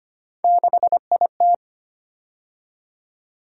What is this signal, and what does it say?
Morse code, heard as a keyed tone, about 25 wpm, 700 Hz. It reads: 6IT
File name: Three Letter-Number Combinations - 25wpm